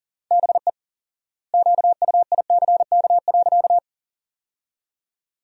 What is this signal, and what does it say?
Morse code, heard as a keyed tone, about 40 wpm, 700 Hz. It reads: BE QUICK.
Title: 4x Word-Spacing - Sentences from Top 500 Words - 40wpm